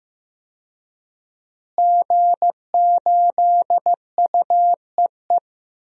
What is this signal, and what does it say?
Morse code, heard as a keyed tone, about 15 wpm, 700 Hz. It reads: G8UEE
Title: DXCC-100 Call Signs - 5 Characters - Repeated 3x - 15wpm